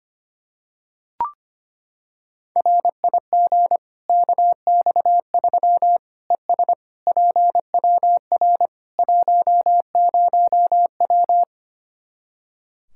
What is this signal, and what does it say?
Morse code, heard as a keyed tone, about 25 wpm, 700 Hz. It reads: RIG KX3 ES PWR 10W
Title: Practice QSO - Exchange 2 - Info - 25wpm